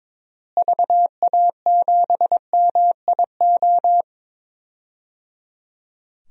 Morse code, heard as a keyed tone, about 22 wpm, 700 Hz. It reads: VA7MIO
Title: DXCC-100 Call Signs - 6 Characters - Repeated 3x - 22wpm